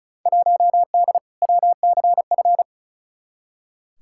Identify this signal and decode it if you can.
Morse code, heard as a keyed tone, about 36 wpm, 700 Hz. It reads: 1D WCF
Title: Field Day Exchanges - 35wpm